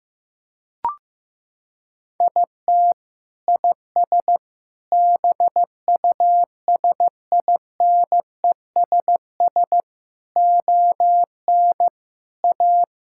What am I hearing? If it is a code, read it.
Morse code, heard as a keyed tone, about 15 wpm, 700 Hz. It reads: IT IS BUSINESS ON A
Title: Sentences from Top 1000 Words - Review Part 5 - 15wpm